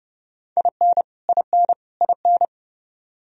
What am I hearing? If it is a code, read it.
Morse code, heard as a keyed tone, about 31 wpm, 700 Hz. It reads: IN IN IN
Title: Top 100 Words Repeated 30wpm